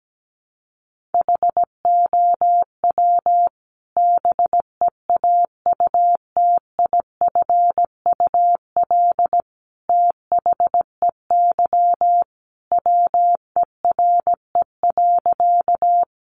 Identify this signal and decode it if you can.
Morse code, heard as a keyed tone, about 17 wpm, 700 Hz. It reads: HOW BEAUTIFUL THEY WERE.
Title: Sentences from Top 900 Words 17wpm